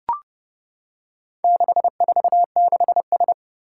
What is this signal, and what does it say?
Morse code, heard as a keyed tone, about 32 wpm, 700 Hz. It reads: 646S
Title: Four Challenging Letter-Number Combinations 30wpm